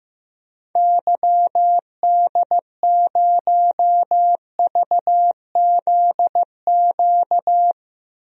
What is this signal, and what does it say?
Morse code, heard as a keyed tone, about 15 wpm, 700 Hz. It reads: YD0VZQ
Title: DXCC-100 Call Signs - 6 Characters - 15wpm